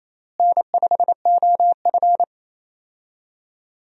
Morse code, heard as a keyed tone, about 28 wpm, 700 Hz. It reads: N5OF
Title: USA Call Signs - 4 Characters - Repeated 3x - 28wpm